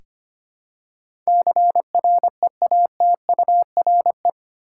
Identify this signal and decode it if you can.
Morse code, heard as a keyed tone, about 25 wpm, 700 Hz. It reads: CREATURE
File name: Eight Letter Words - 25wpm